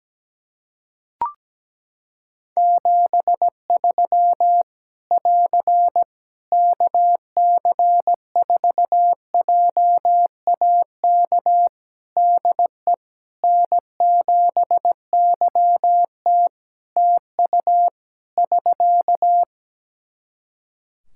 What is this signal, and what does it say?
Morse code, heard as a keyed tone, about 17 wpm, 700 Hz. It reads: 73 <AR> KC4JAK DE N7YT TU <SK>
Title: Practice QSO - Exchange 4 - Ending - 17wpm